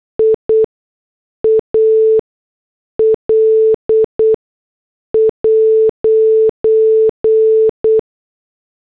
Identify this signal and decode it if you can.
Morse code, heard as a keyed tone, about 8 wpm, spaced as for 6 wpm, 440 Hz. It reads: IAL'